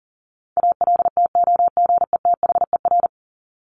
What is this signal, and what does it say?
Morse code, heard as a keyed tone, about 40 wpm, 700 Hz. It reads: ALTOGETHER
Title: Top 1501-1750 Words 40wpm